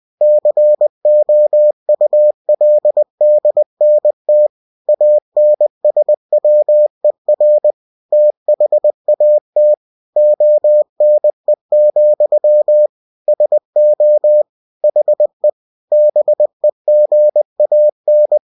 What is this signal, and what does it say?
Morse code, heard as a keyed tone, about 20 wpm, 600 Hz. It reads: COULDNT ANSWER THAT ONE, SO HE BEGAN
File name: cw-books / winnie-the-pooh / winnie-20wpm-page-0081